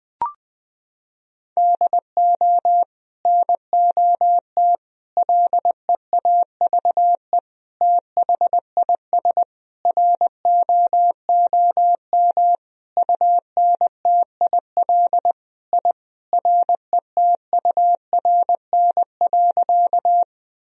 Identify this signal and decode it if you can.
Morse code, heard as a keyed tone, about 20 wpm, 700 Hz. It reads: DO NOT LEAVE THIS ROOM UNTIL I RETURN.
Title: Sentences from Top 1000 Words - Review Part 1 - 20wpm